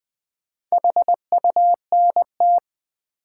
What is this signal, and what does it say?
Morse code, heard as a keyed tone, about 21 wpm, 700 Hz. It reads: HUNT
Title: Sentences from Top 1600 Words - New Word Drill - 20wpm